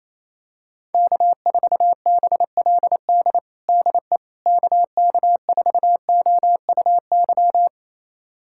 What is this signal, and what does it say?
Morse code, heard as a keyed tone, about 28 wpm, 700 Hz. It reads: K4BLD DE KK4OUY
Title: USA Callsign DE Callsign 28wpm